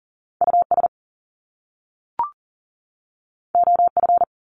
Morse code, heard as a keyed tone, about 39 wpm, 700 Hz. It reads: US OF
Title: Top Two Letter Combinations in Words 40wpm